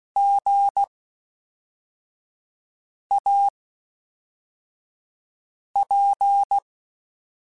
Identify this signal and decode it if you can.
Morse code, heard as a keyed tone, about 16 wpm, 800 Hz. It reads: G A P
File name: Special INGs / Special INGs-mp3s / Fast ing @10